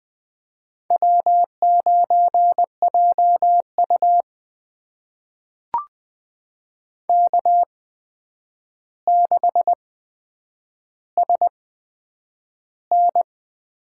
Morse code, heard as a keyed tone, about 20 wpm, spaced as for 9 wpm, 700 Hz. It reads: W9JU K6SN
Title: USA Call Signs - 4 Characters - 8x Character Spacing - 20wpm